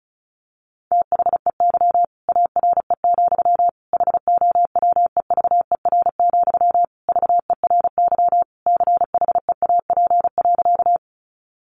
Morse code, heard as a keyed tone, about 34 wpm, 700 Hz. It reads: THEY ARE, HOWEVER, VERY CHEAP.
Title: Sentences from Top 1200 Words 35wpm